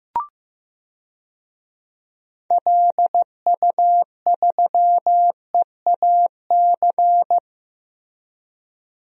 Morse code, heard as a keyed tone, about 15 wpm, 700 Hz. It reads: LU3EAC
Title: DXCC-100 Call Signs - 6 Characters - Repeated 2x - 15wpm